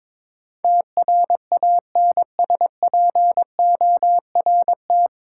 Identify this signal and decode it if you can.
Morse code, heard as a keyed tone, about 21 wpm, 700 Hz. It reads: TRANSPORT
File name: 9 Letter Words - First 3 Letters Given - 22wpm